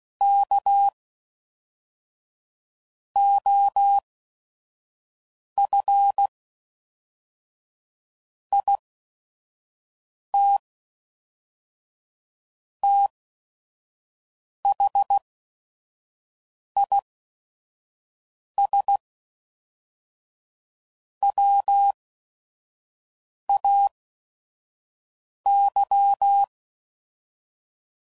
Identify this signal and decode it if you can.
Morse code, heard as a keyed tone, about 16 wpm, spaced as for 5 wpm, 800 Hz. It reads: K OF IT THIS WAY
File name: CW Code Groups @5WPM